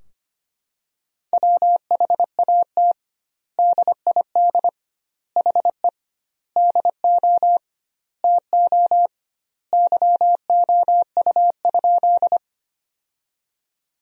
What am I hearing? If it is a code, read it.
Morse code, heard as a keyed tone, about 25 wpm, 700 Hz. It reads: WHAT DID HE DO TO YOU?
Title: 2x Word-Spacing - Sentences from Top 100 Words - 25wpm